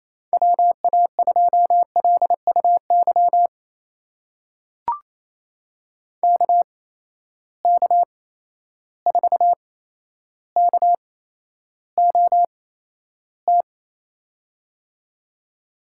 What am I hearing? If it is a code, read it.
Morse code, heard as a keyed tone, about 28 wpm, spaced as for 12 wpm, 700 Hz. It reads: WA2LUY KK4KOT
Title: USA Call Signs - 6 Characters - 8x Character Spacing - 28wpm